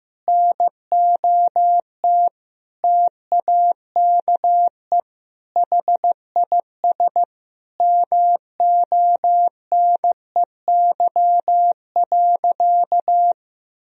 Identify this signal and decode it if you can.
Morse code, heard as a keyed tone, about 16 wpm, 700 Hz. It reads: NOT TAKE HIS MONEY.
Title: Sentences from Top 1000 Words - Review Part 4 - 15wpm